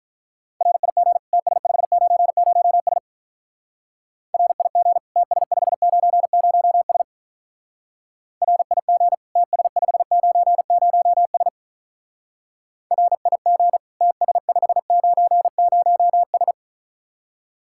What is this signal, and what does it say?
Morse code, heard as a keyed tone, about 43 wpm, 700 Hz. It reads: RIG TS590S RIG TS590S RIG TS590S RIG TS590S
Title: Speed-Racing Common QSO Phrases - 35wpm